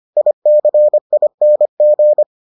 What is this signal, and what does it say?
Morse code, heard as a keyed tone, about 25 wpm, 600 Hz. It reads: ICING